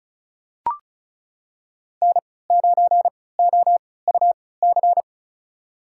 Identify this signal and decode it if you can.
Morse code, heard as a keyed tone, about 36 wpm, spaced as for 20 wpm, 700 Hz. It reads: N9OUC
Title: USA Call Signs - 5 Characters - 3x Character Spacing - 35wpm